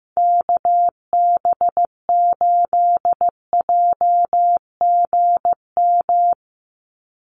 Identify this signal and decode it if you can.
Morse code, heard as a keyed tone, about 15 wpm, 700 Hz. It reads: KB8JGM
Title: USA Call Signs 15wpm